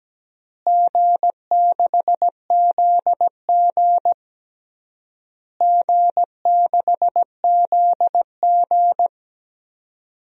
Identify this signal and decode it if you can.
Morse code, heard as a keyed tone, about 17 wpm, 700 Hz. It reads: G6ZG G6ZG
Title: DXCC-100 Call Signs - 4 Characters - Repeated 3x - 17wpm